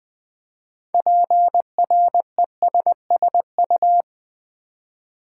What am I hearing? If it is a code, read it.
Morse code, heard as a keyed tone, about 20 wpm, 700 Hz. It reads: PRESSU
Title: Build A Word - 2 Letters At A Time - 8-Letter Words - 20wpm